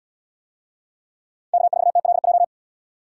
Morse code, heard as a keyed tone, about 75 wpm, 700 Hz. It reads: K4TLJ